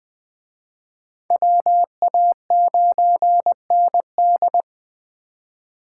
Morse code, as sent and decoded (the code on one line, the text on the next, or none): .-- .- ----. -. -..
WA9ND